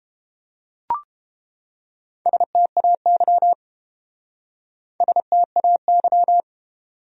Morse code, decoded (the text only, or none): STAY STAY